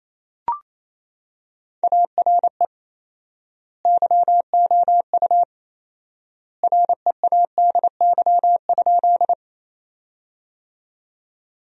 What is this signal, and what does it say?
Morse code, heard as a keyed tone, about 28 wpm, 700 Hz. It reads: ARE YOU READY?
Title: 4x Word-Spacing - Sentences from Top 400 Words - 28wpm